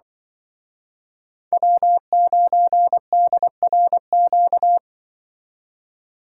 Morse code, heard as a keyed tone, about 23 wpm, 700 Hz. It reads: W9DRQ
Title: Speed Racing - USA 1x3 Call Signs 20wpm